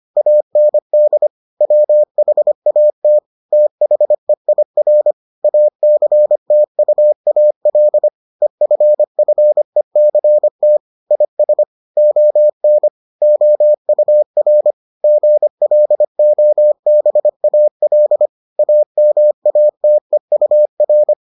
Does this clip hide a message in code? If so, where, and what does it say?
Yes: Morse code from 0.2 to 21.1 s, heard as a keyed tone, about 25 wpm, 600 Hz. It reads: AND WHAT THEIR ACTUAL EFFECT IS ON OUR GLOBAL AMATEUR